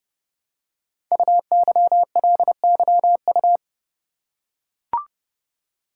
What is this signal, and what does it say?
Morse code, heard as a keyed tone, about 30 wpm, 700 Hz. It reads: UYLYU
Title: Code Groups - 5 Letters - 30wpm